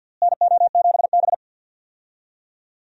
Morse code, heard as a keyed tone, about 50 wpm, 700 Hz. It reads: NO7B
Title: USA Call Signs - 4 Characters - Repeated 2x - 50wpm